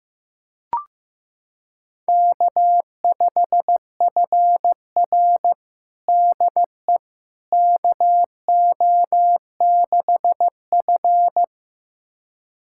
Morse code, heard as a keyed tone, about 15 wpm, 700 Hz. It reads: K5FR DE KO6F
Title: USA Callsign DE Callsign - 4 Characters - 15wpm